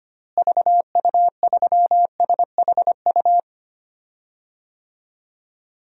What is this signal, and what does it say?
Morse code, heard as a keyed tone, about 25 wpm, 700 Hz. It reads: VU3SHU
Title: DXCC-100 Call Signs - 6 Characters - Repeated 2x - 25wpm